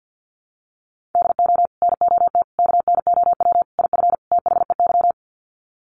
Morse code, heard as a keyed tone, about 48 wpm, 700 Hz. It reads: DO NOT KNOW IF THEY
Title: Sets of 5 Words Encore 50wpm